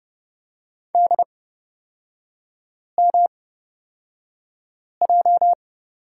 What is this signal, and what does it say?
Morse code, heard as a keyed tone, about 30 wpm, spaced as for 5 wpm, 700 Hz. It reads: DMJ
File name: Lesson 37e - Letter x - Three Characters 30wpm